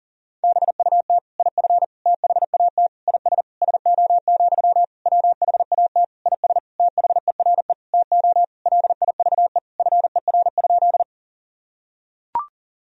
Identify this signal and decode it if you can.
Morse code, heard as a keyed tone, about 37 wpm, 700 Hz. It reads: BUT IF THAT IS SO, WHAT IS THERE TO LIVE FER?
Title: Sentences from Top 200 Words and Abbreviations 40wpm